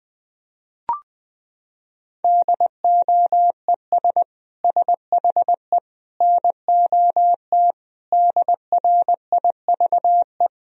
Morse code, heard as a keyed tone, about 20 wpm, 700 Hz. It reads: DOES SHE NOT DRIVE
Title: Sentences from Top 1000 Words - Review Part 2 - 20wpm